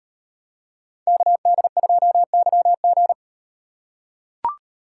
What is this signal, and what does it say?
Morse code, heard as a keyed tone, about 38 wpm, 700 Hz. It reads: KD2YG